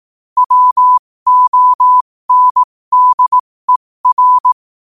Morse code, heard as a keyed tone, about 18 wpm, spaced as for 15 wpm, 1000 Hz. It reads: WONDER
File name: qotdfiles / 15wpm